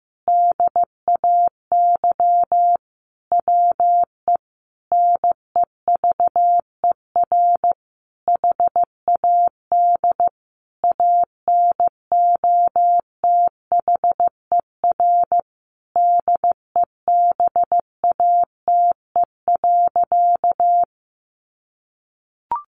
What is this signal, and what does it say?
Morse code, heard as a keyed tone, about 15 wpm, 700 Hz. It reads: DAY WE NEVER HAD ANOTHER DEBATE.
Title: Sentences from Top 1100 Words 15wpm